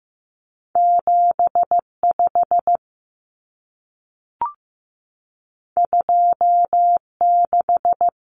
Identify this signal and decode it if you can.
Morse code, heard as a keyed tone, about 15 wpm, 700 Hz. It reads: 75 26
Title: Numbers - 2-Digit 15wpm